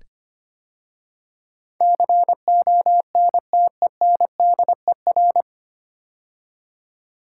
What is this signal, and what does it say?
Morse code, heard as a keyed tone, about 25 wpm, 690 Hz. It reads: CONTENDER
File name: practice / 25wpm / 9 Letter Words - First 3 Letters Given - 25wpm